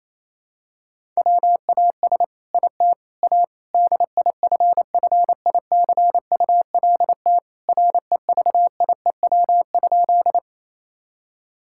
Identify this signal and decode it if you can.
Morse code, heard as a keyed tone, about 28 wpm, 700 Hz. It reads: WAS IT A DIFFICULT REVIEW?
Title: Sentences from Top 1000 Words - Review Part 6 - 28wpm